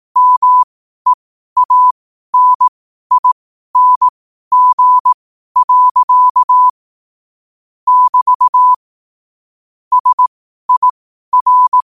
Morse code, heard as a keyed tone, about 18 wpm, spaced as for 12 wpm, 1000 Hz. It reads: MEANING. <BT> SIR
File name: qotdfiles / 12wpm